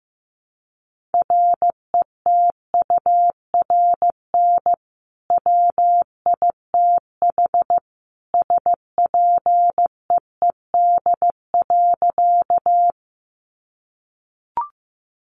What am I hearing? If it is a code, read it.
Morse code, heard as a keyed tone, about 15 wpm, 700 Hz. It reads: RETURN WITH SPEED.
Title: Sentences from Top 800 Words 15wpm